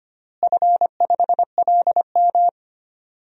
Morse code, heard as a keyed tone, about 25 wpm, 700 Hz. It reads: F5LM
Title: DXCC-100 Call Signs - 4 Characters - Repeated 3x - 25wpm